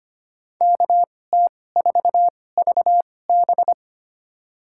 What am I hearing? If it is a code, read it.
Morse code, heard as a keyed tone, about 25 wpm, spaced as for 18 wpm, 700 Hz. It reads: KT4VB
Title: USA Call Signs - 5 Characters - 2x Character Spacing - 25wpm